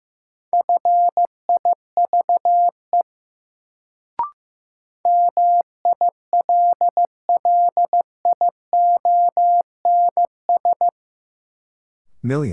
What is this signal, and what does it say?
Morse code, heard as a keyed tone, about 15 wpm, 700 Hz. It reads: FIVE MILLIONS
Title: Sets of 5 Words - New Word Drill - 15wpm